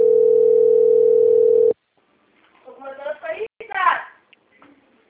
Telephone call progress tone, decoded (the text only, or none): ringback tone